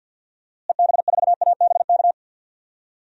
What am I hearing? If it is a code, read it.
Morse code, heard as a keyed tone, about 47 wpm, 700 Hz. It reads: EB4ABK
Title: DXCC-100 Call Signs - 6 Characters - Repeated 3x - 50wpm